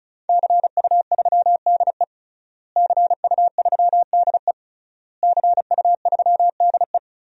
Morse code, heard as a keyed tone, about 34 wpm, 700 Hz. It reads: CU3DE CU3DE CU3DE
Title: DXCC-100 Call Signs - 5 Characters - Repeated 3x - 35wpm